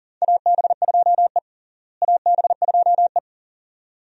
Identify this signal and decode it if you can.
Morse code, heard as a keyed tone, about 40 wpm, 700 Hz. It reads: AB2E AB2E
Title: USA Call Signs - 4 Characters - Repeated 2x - 40wpm